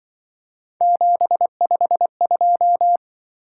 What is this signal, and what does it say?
Morse code, heard as a keyed tone, about 25 wpm, 700 Hz. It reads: 752
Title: Speed-Racing - Three Letter-Number Combinations - 20wpm